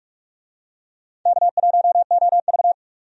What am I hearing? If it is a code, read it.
Morse code, heard as a keyed tone, about 46 wpm, 700 Hz. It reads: K1OV
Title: USA Call Signs - 4 Characters - 45wpm